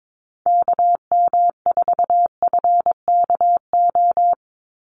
Morse code, heard as a keyed tone, about 22 wpm, 700 Hz. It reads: KM4FKO
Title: USA Call Signs 22wpm